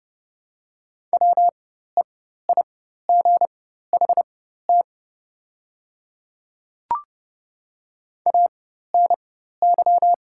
Morse code, heard as a keyed tone, about 30 wpm, spaced as for 14 wpm, 690 Hz. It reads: WEIGHT ANY
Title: 4x Character Spacing - Top 500 Words - 30wpm